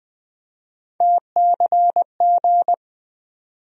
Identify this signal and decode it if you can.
Morse code, heard as a keyed tone, about 20 wpm, 700 Hz. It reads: TCG
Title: USA Call Sign Postfixes - 3 Letters 20wpm